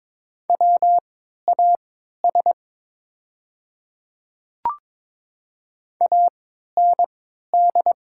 Morse code, heard as a keyed tone, about 22 wpm, spaced as for 12 wpm, 700 Hz. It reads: WAS AND
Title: 3x Character Spacing - Top 100 Words - 22wpm